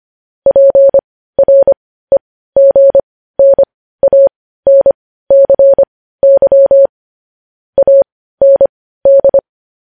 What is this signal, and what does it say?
Morse code, heard as a keyed tone, about 25 wpm, spaced as for 15 wpm, 550 Hz. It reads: PREGNANCY AND
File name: podcast / News headlines-1x-25x15